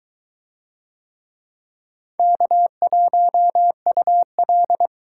Morse code, heard as a keyed tone, about 21 wpm, 700 Hz. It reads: K1UL